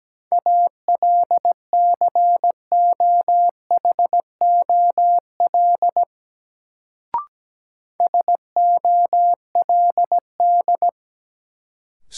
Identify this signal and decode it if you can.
Morse code, heard as a keyed tone, about 17 wpm, 700 Hz. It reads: ALCOHOL SOLD
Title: Sentences from Top 1400 Words - New Word Drill Alt - 17wpm